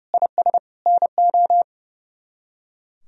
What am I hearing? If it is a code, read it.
Morse code, heard as a keyed tone, about 35 wpm, 700 Hz. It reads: IS NO
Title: Speed-Racing - Sets of 4 Words 30wpm